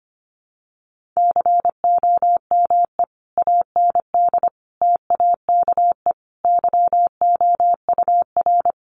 Morse code, heard as a keyed tone, about 25 wpm, 700 Hz. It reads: COME AND TAKE YOUR
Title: Sentences from Top 1500 Words 25wpm